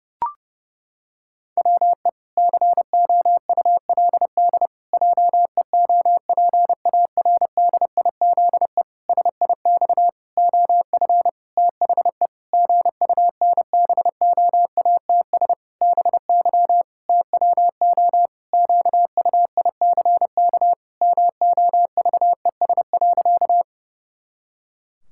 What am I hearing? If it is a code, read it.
Morse code, heard as a keyed tone, about 32 wpm, 700 Hz. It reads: WE COULD JEOPARDIZE SIX OF THE GUNBOATS BY TWO QUICK MOVES.
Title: Pangram Challenge v2 - 30wpm